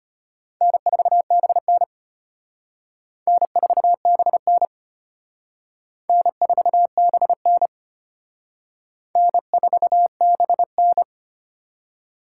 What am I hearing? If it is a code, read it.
Morse code, heard as a keyed tone, about 31 wpm, 700 Hz. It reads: N4BN N4BN N4BN N4BN
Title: Speed Racing - USA 1x2 Call Signs 25wpm